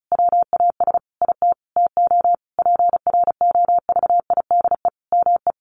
Morse code, heard as a keyed tone, about 34 wpm, 700 Hz. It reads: WAS IT TO PROVIDE ME
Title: Sentences from Top 1500 Words 35wpm